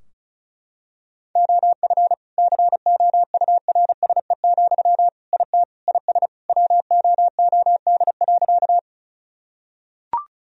Morse code, heard as a keyed tone, about 34 wpm, 700 Hz. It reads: OF COURSE, IT IS WOOD.
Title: Sentences from Top 1000 Words - Review Part 2 - 35wpm